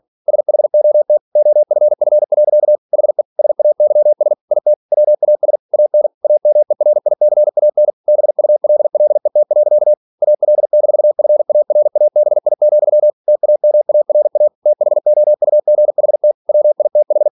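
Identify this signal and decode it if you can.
Morse code, heard as a keyed tone, about 46 wpm, 600 Hz. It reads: SHOT OFF. HE SAYS IT WAS AN AMERICAN BULLET. AL-FARABI, TAMARA THOUGHT WITH
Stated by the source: Practice text from an English novel sent at 47 WPM.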